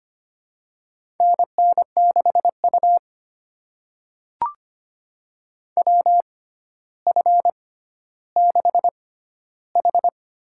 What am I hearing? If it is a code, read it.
Morse code, heard as a keyed tone, about 25 wpm, spaced as for 13 wpm, 700 Hz. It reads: NN6U WF6H
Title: USA Call Signs - 4 Characters - 6x Character Spacing - 25wpm